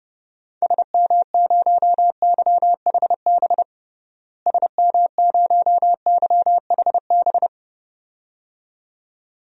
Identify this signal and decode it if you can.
Morse code, heard as a keyed tone, about 32 wpm, 700 Hz. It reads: SM0YHB SM0YHB
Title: DXCC-100 Call Signs - 6 Characters - Repeated 2x - 30wpm